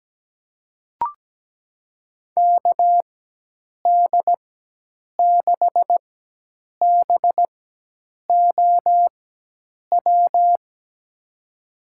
Morse code, heard as a keyed tone, about 17 wpm, spaced as for 8 wpm, 700 Hz. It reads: KD6BOW
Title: USA Call Signs - 4x Character Spacing - 17wpm